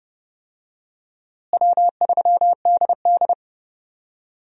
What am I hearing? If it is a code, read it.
Morse code, heard as a keyed tone, about 30 wpm, 700 Hz. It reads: W3DD
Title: USA Call Signs - 4 Characters - Repeated 2x - 30wpm